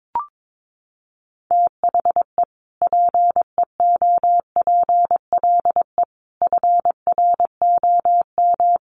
Morse code, heard as a keyed tone, about 22 wpm, 700 Hz. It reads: THE PEOPLE FROM
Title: Sentences from Top 600 Words 22wpm